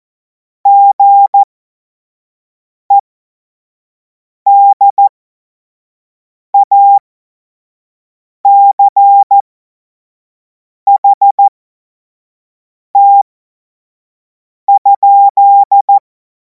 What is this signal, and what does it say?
Morse code, heard as a keyed tone, about 14 wpm, spaced as for 5 wpm, 800 Hz. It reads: GEDACHT?
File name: Morsekurs / Witze / Witz 03. März 2026 25 F70